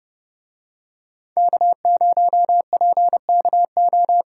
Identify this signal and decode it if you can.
Morse code, heard as a keyed tone, about 31 wpm, 700 Hz. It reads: K0PKO